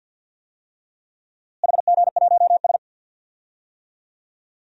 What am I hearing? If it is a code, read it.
Morse code, heard as a keyed tone, about 52 wpm, 700 Hz. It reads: HG1S
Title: DXCC-100 Call Signs - 4 Characters - Repeated 3x - 50wpm